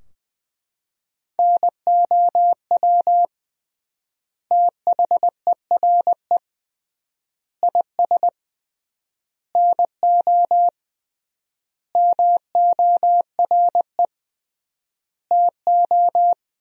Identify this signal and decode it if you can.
Morse code, heard as a keyed tone, about 21 wpm, 700 Hz. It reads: NOW THERE IS NO MORE TO
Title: 3x Word-Spacing - Sentences from Top 100 Words - 20wpm